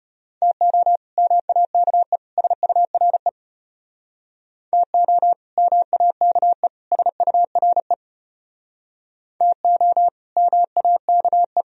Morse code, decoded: TO MAKE SURE TO MAKE SURE TO MAKE

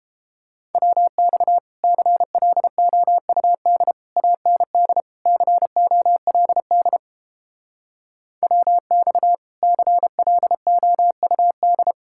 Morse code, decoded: WX CLOUD AND COLD WX CLOUD

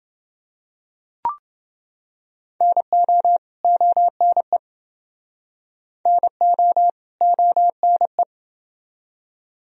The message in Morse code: NO ONE NO ONE

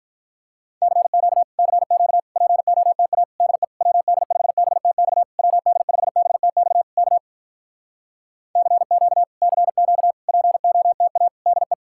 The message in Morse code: CQ CQ POTA DE WD5BTX WD5BTX K CQ CQ POTA DE

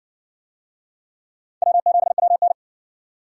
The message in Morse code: W7WN

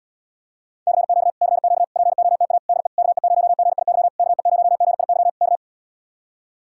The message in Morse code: CQ CQ POTA DE KE0GIY KE0GIY K

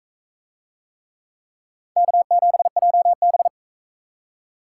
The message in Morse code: K7JB

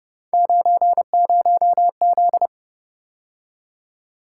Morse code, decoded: 90Z

